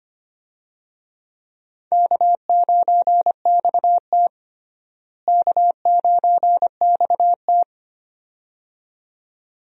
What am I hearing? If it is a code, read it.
Morse code, heard as a keyed tone, about 25 wpm, 700 Hz. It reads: K9XT K9XT
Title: USA Call Signs - 4 Characters - Repeated 2x - 25wpm